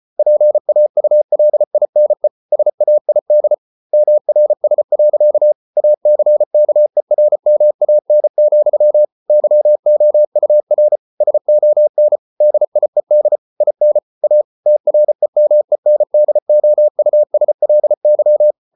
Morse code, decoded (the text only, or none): PAULINE SAID MRS. ACKERMAN, YOUR SON DIED IN A TREMENDOUSLY